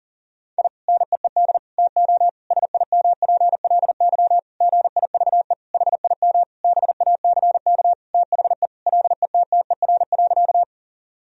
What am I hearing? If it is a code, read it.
Morse code, heard as a keyed tone, about 40 wpm, 700 Hz. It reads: I NEED TO SIMPLY GIVE HIM BACK THE LETTER.
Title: Sentences from Top 1000 Words - Review Part 4 - 40wpm